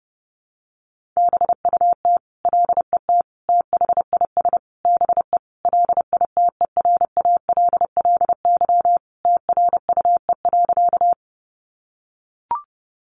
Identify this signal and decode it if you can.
Morse code, heard as a keyed tone, about 30 wpm, 700 Hz. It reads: BUT LET THIS BE LITERALLY TRUE.
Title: Sentences from Top 1400 Words 30wpm